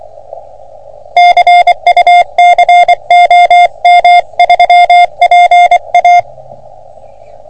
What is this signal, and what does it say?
Morse code, heard as a keyed tone, about 24 wpm, 700 Hz. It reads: CUC OM3PA
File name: OM3PA PILUP CUC 388